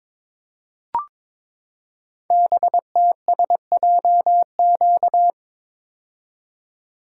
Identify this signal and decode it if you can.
Morse code, heard as a keyed tone, about 22 wpm, 700 Hz. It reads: BTSJQ
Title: Code Groups - 5 Letters - 22wpm